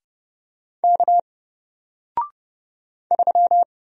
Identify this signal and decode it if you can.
Morse code, heard as a keyed tone, about 30 wpm, 700 Hz. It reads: K 3
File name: Single-Letter-Number-30wpm